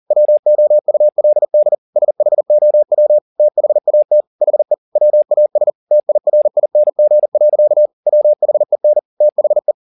Morse code, heard as a keyed tone, about 40 wpm, 600 Hz. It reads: WOULD SHOW THAT HE WAS TIRING. WHEN THE